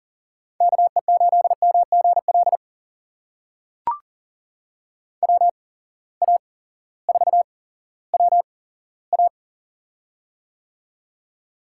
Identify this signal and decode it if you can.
Morse code, heard as a keyed tone, about 40 wpm, spaced as for 18 wpm, 690 Hz. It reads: KE8MGL WA4WA
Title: USA Call Signs - 8x Character Spacing - 40wpm